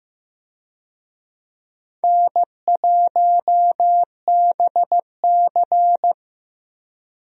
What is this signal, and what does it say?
Morse code, heard as a keyed tone, about 15 wpm, 700 Hz. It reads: N1BC
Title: USA Call Signs - 4 Characters - 15wpm